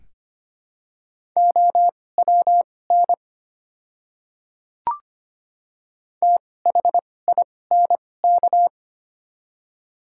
Morse code, heard as a keyed tone, about 25 wpm, spaced as for 18 wpm, 700 Hz. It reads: OWN THINK
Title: 2x Character Spacing - Top 300 Words - 25wpm